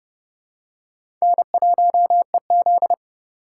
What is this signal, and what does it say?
Morse code, heard as a keyed tone, about 30 wpm, 700 Hz. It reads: N1EZ
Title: USA Call Signs - 4x Character Spacing - 30wpm